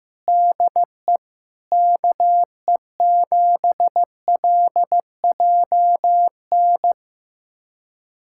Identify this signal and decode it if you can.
Morse code, heard as a keyed tone, about 15 wpm, 700 Hz. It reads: DE KE7LJN
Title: USA Callsign DE Callsign 15wpm